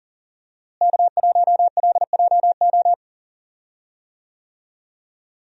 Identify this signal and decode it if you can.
Morse code, heard as a keyed tone, about 40 wpm, 700 Hz. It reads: K1LJO
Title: USA Call Signs - 5 Characters - Repeated 2x - 40wpm